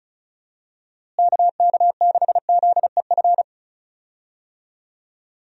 Morse code, heard as a keyed tone, about 34 wpm, 700 Hz. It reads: KK6ZEF
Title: USA Call Signs - 6 Characters - Repeated 3x - 35wpm